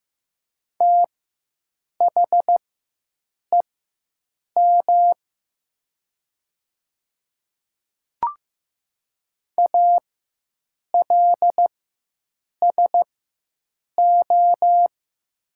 Morse code, heard as a keyed tone, about 15 wpm, spaced as for 7 wpm, 700 Hz. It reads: THEM ALSO
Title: 4x Character Spacing - Top 300 Words - 15wpm